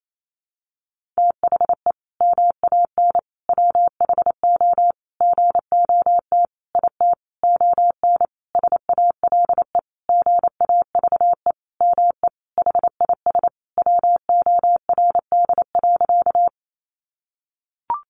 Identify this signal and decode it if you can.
Morse code, heard as a keyed tone, about 28 wpm, 700 Hz. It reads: THE MAN WHO GOT IT ON SALE GAVE ME HIS WORD.
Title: Sentences from Top 1400 Words 28wpm